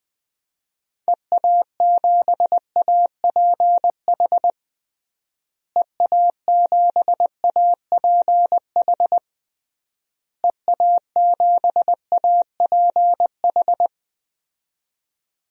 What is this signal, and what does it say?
Morse code, heard as a keyed tone, about 21 wpm, 700 Hz. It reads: EA7APH EA7APH EA7APH
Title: DXCC-100 Call Signs - 6 Characters - Repeated 3x - 20wpm